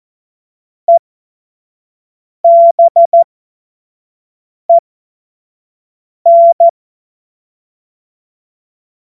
Morse code, heard as a keyed tone, about 14 wpm, spaced as for 5 wpm, 680 Hz. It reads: EBEN
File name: Morsekurs / Witze / Witz 25. Oktober 2022 25 F70